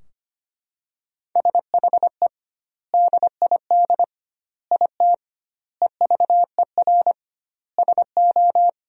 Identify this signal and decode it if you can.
Morse code, heard as a keyed tone, about 25 wpm, 700 Hz. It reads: SHE DID IT EVER SO
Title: 2x Word-Spacing - Sentences from Top 400 Words - 25wpm